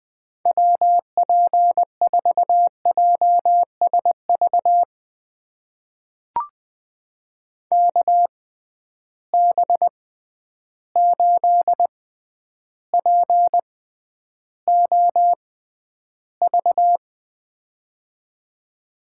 Morse code, heard as a keyed tone, about 20 wpm, spaced as for 10 wpm, 700 Hz. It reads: WP4JSV KB8POV
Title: USA Call Signs - 6 Characters - 6x Character Spacing - 20wpm